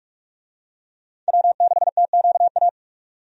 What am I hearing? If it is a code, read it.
Morse code, heard as a keyed tone, about 48 wpm, 700 Hz. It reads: W6TQA
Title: USA Call Signs - 5 Characters - 45wpm